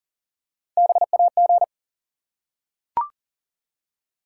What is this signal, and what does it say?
Morse code, heard as a keyed tone, about 42 wpm, 700 Hz. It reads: BAG
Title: Three Letter Words - 40wpm